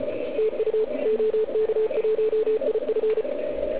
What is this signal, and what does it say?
Morse code, heard as a keyed tone, about 33 wpm, 440 Hz. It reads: TU OK1IF